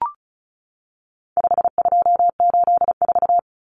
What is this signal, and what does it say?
Morse code, heard as a keyed tone, about 34 wpm, 700 Hz. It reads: 5284
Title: Common Cut Numbers - 4-Digit 35wpm